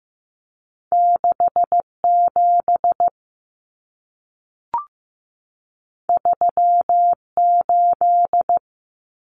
Morse code, heard as a keyed tone, about 15 wpm, 700 Hz. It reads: 67 38
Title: Common Cut Numbers - 2-Digit 15wpm